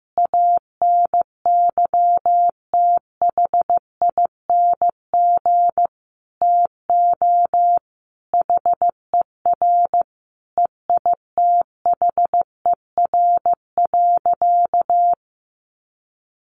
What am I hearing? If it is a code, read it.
Morse code, heard as a keyed tone, about 15 wpm, 700 Hz. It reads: ANYTHING TO HER EITHER.
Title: Sentences from Top 1000 Words 15wpm